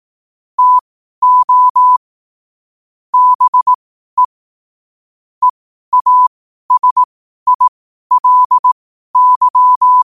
Morse code, heard as a keyed tone, about 18 wpm, spaced as for 12 wpm, 1000 Hz. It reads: TO BE EASILY
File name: qotdfiles / 12wpm